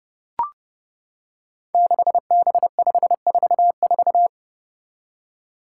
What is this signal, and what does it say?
Morse code, heard as a keyed tone, about 30 wpm, 700 Hz. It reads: 6B544
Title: Five Challenging Letter-Number Combinations 30wpm